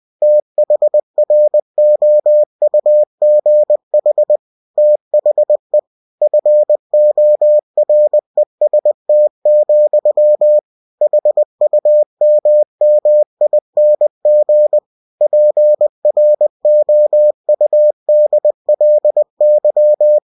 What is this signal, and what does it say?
Morse code, heard as a keyed tone, about 20 wpm, 600 Hz. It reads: THROUGH THE FOREST, HUMMING PROUDLY
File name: cw-books / winnie-the-pooh / winnie-20wpm-page-0020